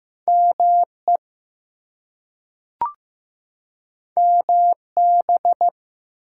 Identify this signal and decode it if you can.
Morse code, heard as a keyed tone, about 15 wpm, 700 Hz. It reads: ME MB
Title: US States and Canadian Provinces 15wpm